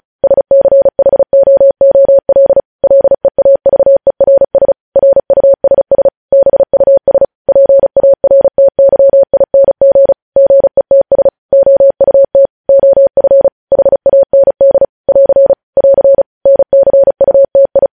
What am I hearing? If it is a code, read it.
Morse code, heard as a keyed tone, about 36 wpm, 550 Hz. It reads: SCHOOL LEAVERS RUSS BUS PARTYING GETS OUT OF HAND <AR> <AR> NCUTI
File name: podcast / News headlines-1x-35x35